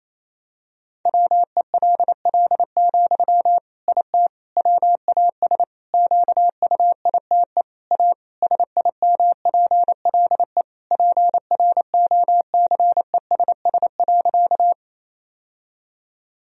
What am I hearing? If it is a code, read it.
Morse code, heard as a keyed tone, about 28 wpm, 700 Hz. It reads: WELL, IT WAS QUITE A SIMPLE PROCESS.
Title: Sentences from Top 1000 Words - Review Part 3 - 28wpm